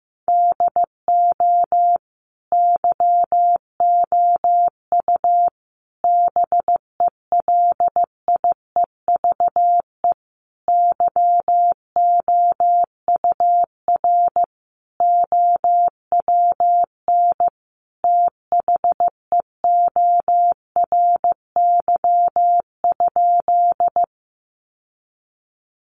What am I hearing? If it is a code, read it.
Morse code, heard as a keyed tone, about 15 wpm, 700 Hz. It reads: DO YOU BELIEVE YOUR OWN THEORY?
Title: Sentences from Top 1000 Words 15wpm